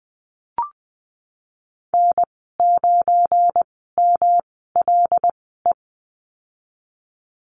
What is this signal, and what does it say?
Morse code, heard as a keyed tone, about 19 wpm, spaced as for 14 wpm, 700 Hz. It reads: N9MLE
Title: 186 Lesson 30g - Letter b - Call Signs 20wpm